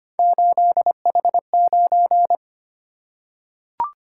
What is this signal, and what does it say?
Morse code, heard as a keyed tone, about 25 wpm, 700 Hz. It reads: 8H9
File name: Three Letter-Number Combinations - 25wpm